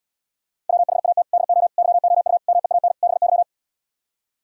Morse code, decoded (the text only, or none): WHAT DO YOU MEAN BY